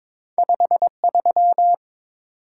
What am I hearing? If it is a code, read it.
Morse code, heard as a keyed tone, about 22 wpm, 700 Hz. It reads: 53
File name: practice / 22wpm / Two Letter-Number Combinations - 22wpm